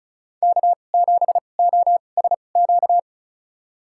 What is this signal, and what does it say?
Morse code, heard as a keyed tone, about 34 wpm, spaced as for 25 wpm, 700 Hz. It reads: K7OSQ
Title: USA Call Signs - 5 Characters - 2x Character Spacing - 35wpm